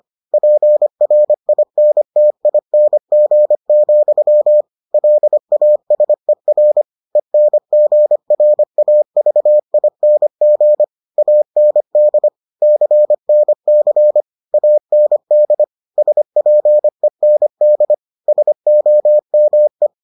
Morse code, heard as a keyed tone, about 25 wpm, 600 Hz. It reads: PRINTING, LASER ENGRAVING AND CNC AND SPEND SOME